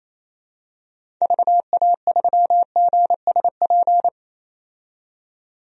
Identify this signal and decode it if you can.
Morse code, heard as a keyed tone, about 28 wpm, 700 Hz. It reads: VA3GSP
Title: DXCC-100 Call Signs - 6 Characters - 28wpm